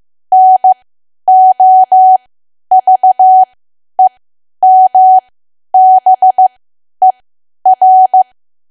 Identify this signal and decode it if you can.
Morse code, heard as a keyed tone, about 15 wpm, spaced as for 10 wpm, 750 Hz. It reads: NOVEMBER
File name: cwa / W1AW-Level2 / 10WPM